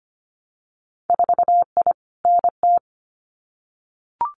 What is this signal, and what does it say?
Morse code, heard as a keyed tone, about 25 wpm, 690 Hz. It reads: 4I NT
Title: Winter Field Day Exchanges Encore 25wpm